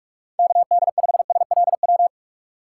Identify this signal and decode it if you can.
Morse code, heard as a keyed tone, about 46 wpm, 700 Hz. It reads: KD5SLW
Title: USA Call Signs - 6 Characters - 4x Character Spacing - 45wpm